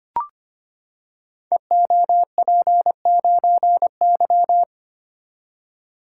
Morse code, heard as a keyed tone, about 25 wpm, 700 Hz. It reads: EOP9Y
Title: Code Groups - 5 Letter:Numbers - 25wpm